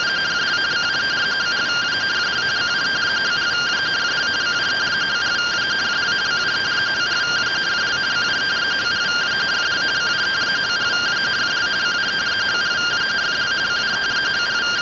Two-tone teleprinter signal, about 170 Hz shift.